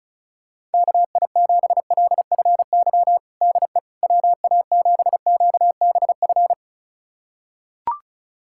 Morse code, heard as a keyed tone, about 36 wpm, 700 Hz. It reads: KI7LFY DE WA7QBF
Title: USA Callsign DE Callsign 35wpm